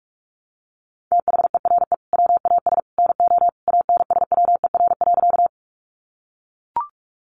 Morse code, heard as a keyed tone, about 44 wpm, 700 Hz. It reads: THERE WAS NO ANSWER.
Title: Sentences from Top 300 Words 45wpm